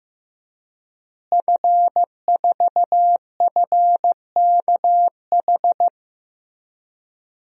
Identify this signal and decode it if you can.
Morse code, heard as a keyed tone, about 15 wpm, 700 Hz. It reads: F4FKH
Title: DXCC-100 Call Signs - 5 Characters - Repeated 3x - 15wpm